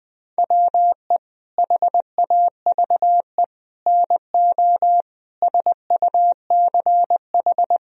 Morse code, heard as a keyed tone, about 21 wpm, 700 Hz. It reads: WE HAVE NO SUCH
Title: Sentences from Top 1000 Words - Review Part 1 - 20wpm